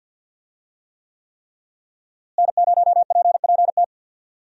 Morse code, heard as a keyed tone, about 50 wpm, 700 Hz. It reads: N0PPT
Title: USA Call Signs - 5 Characters - Repeated 3x - 50wpm